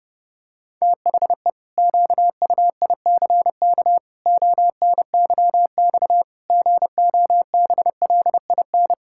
Morse code, heard as a keyed tone, about 30 wpm, 700 Hz. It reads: THE QUICK ONYX GOBLIN
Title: Pangram Challenge v2 - 30wpm